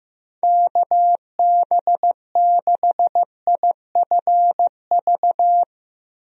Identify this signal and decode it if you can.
Morse code, heard as a keyed tone, about 15 wpm, 700 Hz. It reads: KB6IFV
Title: USA Call Signs - 6 Characters - Repeated 3x - 15wpm